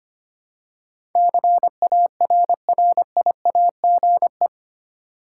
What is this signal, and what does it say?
Morse code, heard as a keyed tone, about 25 wpm, 700 Hz. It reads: CARRIAGE
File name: 8 Letter Words - First 3 Letters Given - 25wpm